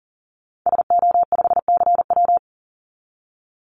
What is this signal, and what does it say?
Morse code, heard as a keyed tone, about 40 wpm, 700 Hz. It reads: SO5CW
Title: Call Signs Repeated 2x - 40wpm